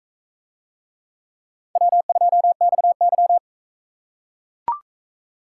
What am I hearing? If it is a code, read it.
Morse code, heard as a keyed tone, about 44 wpm, 700 Hz. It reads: W2XY